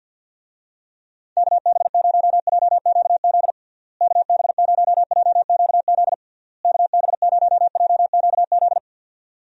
Morse code, heard as a keyed tone, about 49 wpm, 700 Hz. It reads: KB0JQZ KB0JQZ KB0JQZ
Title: USA Call Signs - 6 Characters - Repeated 3x - 50wpm